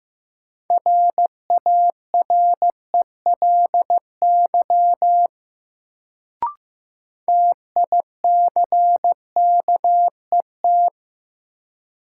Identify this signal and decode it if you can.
Morse code, heard as a keyed tone, about 16 wpm, 700 Hz. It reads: RARELY TICKET
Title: Sentences from Top 1400 Words - New Word Drill Alt - 15wpm